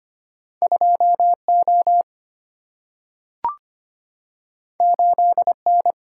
Morse code, heard as a keyed tone, about 25 wpm, 700 Hz. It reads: 2 O 8 N
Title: Single-Letter Single-Number 25wpm